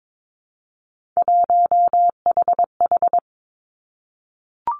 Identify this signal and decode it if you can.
Morse code, heard as a keyed tone, about 22 wpm, 700 Hz. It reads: 1HH
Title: Three Challenging Letter-Number Combinations 22wpm